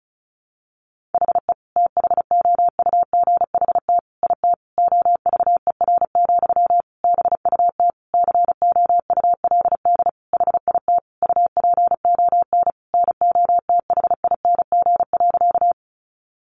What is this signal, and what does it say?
Morse code, heard as a keyed tone, about 34 wpm, 700 Hz. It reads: HE THOUGHT IT OVER, BUT COULD HIT UPON NOTHING.
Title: Sentences from Top 1200 Words 35wpm